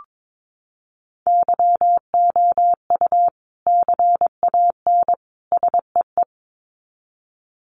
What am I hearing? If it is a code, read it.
Morse code, heard as a keyed tone, about 22 wpm, 700 Hz. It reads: YOU CAN SEE
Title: Sets of 3 Words 22wpm